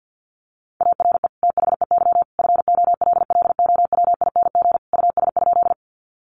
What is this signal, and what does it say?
Morse code, heard as a keyed tone, about 50 wpm, 700 Hz. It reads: ARE THEY FOLLOWING US?
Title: Sentences from Top 700 Words 50wpm